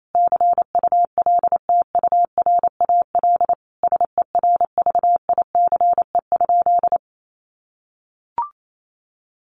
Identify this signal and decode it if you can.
Morse code, heard as a keyed tone, about 28 wpm, 700 Hz. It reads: CULTURAL SERVICE?
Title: Sentences from Top 1500 Words 28wpm